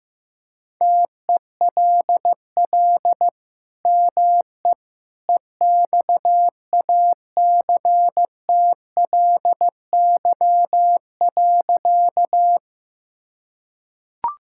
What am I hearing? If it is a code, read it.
Morse code, heard as a keyed tone, about 15 wpm, 700 Hz. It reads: TELL ME EXACTLY.
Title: Sentences from Top 1000 Words - Review Part 6 - 15wpm